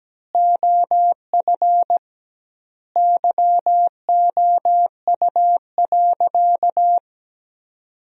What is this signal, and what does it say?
Morse code, heard as a keyed tone, about 17 wpm, 700 Hz. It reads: OF YOU.
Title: 2x Word-Spacing - Sentences from Top 200 Words - 17wpm